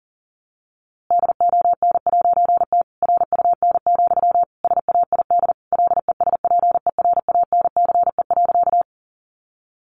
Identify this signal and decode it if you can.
Morse code, heard as a keyed tone, about 40 wpm, 700 Hz. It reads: DON'T RUN, SAID LESPERANCE.
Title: A Sound Of Thunder 40wpm